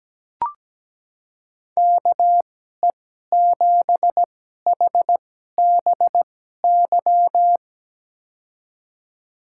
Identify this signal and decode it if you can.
Morse code, heard as a keyed tone, about 17 wpm, spaced as for 12 wpm, 700 Hz. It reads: KE7HBY
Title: USA Call Signs - 2x Character Spacing - 17wpm